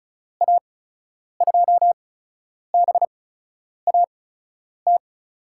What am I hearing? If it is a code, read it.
Morse code, heard as a keyed tone, about 36 wpm, spaced as for 10 wpm, 700 Hz. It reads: A2BAT
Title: USA Call Signs - 6 Characters - 8x Character Spacing - 35wpm